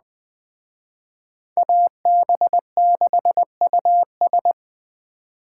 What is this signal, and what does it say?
Morse code, heard as a keyed tone, about 20 wpm, 700 Hz. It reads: AB6US
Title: Speed Racing - USA 2x2 Call Signs 17wpm